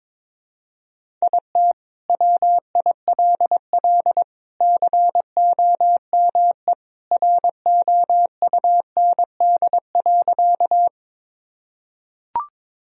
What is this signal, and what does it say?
Morse code, heard as a keyed tone, about 22 wpm, 700 Hz. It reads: IT WILL COME ROUND.
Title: Sentences from Top 1000 Words - Review Part 1 - 22wpm